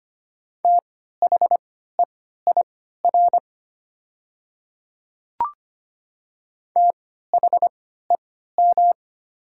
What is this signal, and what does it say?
Morse code, heard as a keyed tone, about 25 wpm, spaced as for 14 wpm, 700 Hz. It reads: THEIR THEM
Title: 3x Character Spacing - Top 100 Words - 25wpm